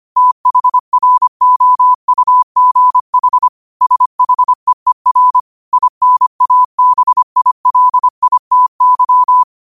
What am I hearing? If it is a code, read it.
Morse code, heard as a keyed tone, about 25 wpm, 1000 Hz. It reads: THROUGH SHEER INABILITY